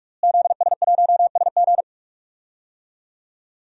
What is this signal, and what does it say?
Morse code, heard as a keyed tone, about 46 wpm, 700 Hz. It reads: ZS1SG
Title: DXCC-100 Call Signs - 5 Characters - Repeated 2x - 45wpm